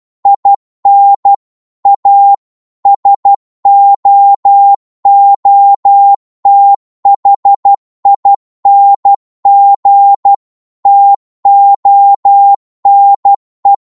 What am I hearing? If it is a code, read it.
Morse code, heard as a keyed tone, about 12 wpm, 800 Hz. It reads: IN A SOOTHING TONE